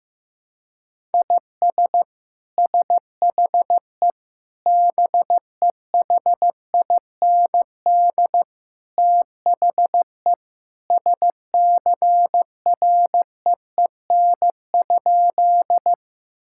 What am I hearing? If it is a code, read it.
Morse code, heard as a keyed tone, about 16 wpm, 700 Hz. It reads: IS SHE BEHIND THE SCREEN?
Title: Sentences from Top 1000 Words - Review Part 6 - 15wpm